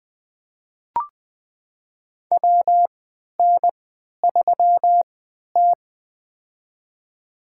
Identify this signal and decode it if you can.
Morse code, heard as a keyed tone, about 21 wpm, spaced as for 12 wpm, 700 Hz. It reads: WN3T
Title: USA Call Signs - 3x Character Spacing - 20wpm